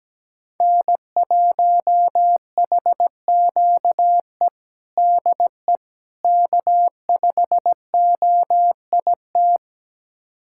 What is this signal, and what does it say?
Morse code, heard as a keyed tone, about 17 wpm, 700 Hz. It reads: N1HQE DE K5OIT
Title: USA Callsign DE Callsign - 5 Characters - 17wpm